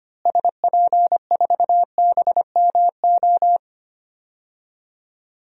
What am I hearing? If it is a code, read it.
Morse code, heard as a keyed tone, about 25 wpm, 700 Hz. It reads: SP4BMO
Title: DXCC-100 Call Signs - 6 Characters - Repeated 3x - 25wpm